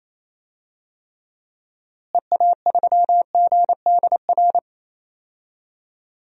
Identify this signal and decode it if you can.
Morse code, heard as a keyed tone, about 28 wpm, 700 Hz. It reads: EA3GDR
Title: DXCC-100 Call Signs - 6 Characters - Repeated 2x - 28wpm